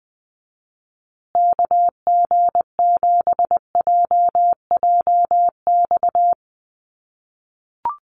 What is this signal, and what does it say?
Morse code, heard as a keyed tone, about 20 wpm, 700 Hz. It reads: KG7JJX
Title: USA Call Signs Repeated 2x 20wpm